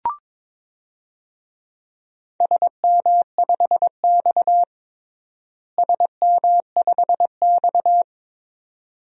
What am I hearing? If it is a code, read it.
Morse code, heard as a keyed tone, about 22 wpm, 700 Hz. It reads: SM5X SM5X
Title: DXCC-100 Call Signs - 4 Characters - Repeated 2x - 22wpm